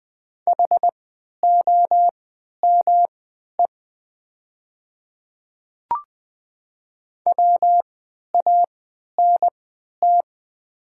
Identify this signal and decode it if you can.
Morse code, heard as a keyed tone, about 20 wpm, spaced as for 11 wpm, 700 Hz. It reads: HOME WANT
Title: 3x Character Spacing - Top 200 Words - 20wpm